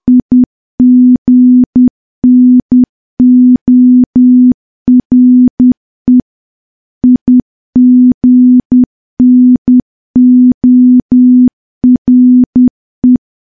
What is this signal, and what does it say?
Morse code, heard as a keyed tone, about 10 wpm, 250 Hz. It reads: IGNORE IGNORE
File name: morse / morse-training-10WPM